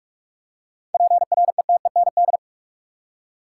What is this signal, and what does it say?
Morse code, heard as a keyed tone, about 46 wpm, 700 Hz. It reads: PRETEND
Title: Seven Letter Words - 45wpm